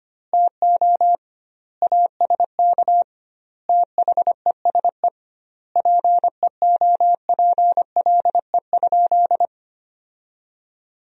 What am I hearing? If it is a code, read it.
Morse code, heard as a keyed tone, about 25 wpm, 700 Hz. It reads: TO ASK THESE PEOPLE?
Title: 2x Word-Spacing - Sentences from Top 200 Words - 25wpm